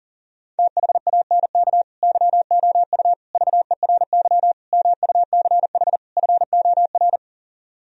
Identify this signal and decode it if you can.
Morse code, heard as a keyed tone, about 37 wpm, 700 Hz. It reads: THANK YOU VERY MUCH FOR
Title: Speed-Racing - Sets of 5 Words 40wpm